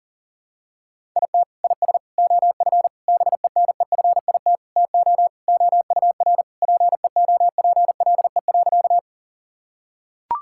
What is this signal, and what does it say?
Morse code, heard as a keyed tone, about 40 wpm, 700 Hz. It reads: IT IS OF BENEFIT TO OUR PEOPLE.
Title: Sentences from Top 1000 Words - Review Part 6 - 40wpm